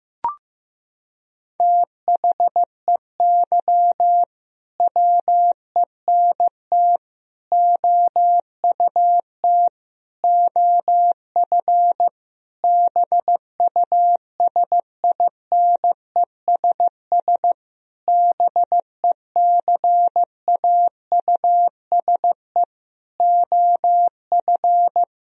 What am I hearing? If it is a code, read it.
Morse code, heard as a keyed tone, about 15 wpm, 700 Hz. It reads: THEY WENT OUT OF BUSINESS BECAUSE OF
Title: Sentences from Top 1000 Words - Review Part 4 - 15wpm